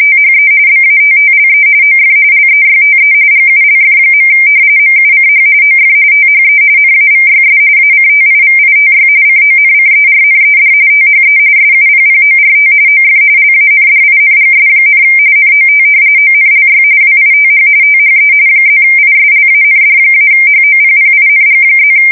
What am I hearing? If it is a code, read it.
RTTY, about 45.45 baud. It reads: THE BATTLE OF LEYTE GULF. MUCH OF THE NAVYS ACTIVITY CONCERNED THE SUPPORT OF LANDINGS, NOT ONLY WITH THE +ISLAND-HOPPING+
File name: fsk170 / usnavy1